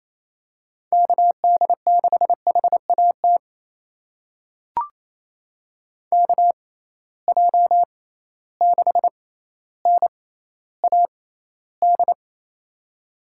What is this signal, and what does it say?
Morse code, heard as a keyed tone, about 28 wpm, spaced as for 14 wpm, 700 Hz. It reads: KD6HAT KJ6NAD
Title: USA Call Signs - 6 Characters - 6x Character Spacing - 28wpm